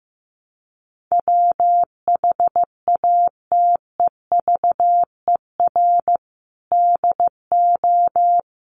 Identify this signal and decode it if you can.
Morse code, heard as a keyed tone, about 15 wpm, 700 Hz. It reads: WHATEVER DO
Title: Sentences from Top 1100 Words 15wpm